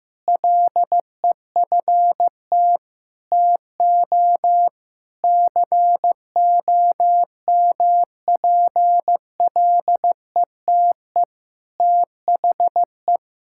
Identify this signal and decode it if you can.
Morse code, heard as a keyed tone, about 15 wpm, 700 Hz. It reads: LEFT TO COMPLETE THE
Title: Sentences from Top 1000 Words - Review Part 6 - 15wpm